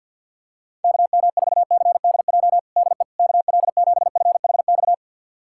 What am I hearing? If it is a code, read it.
Morse code, heard as a keyed tone, about 47 wpm, 700 Hz. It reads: KM4CDJ DE KL7FHX
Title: USA Callsign DE Callsign - 6 Characters - 50wpm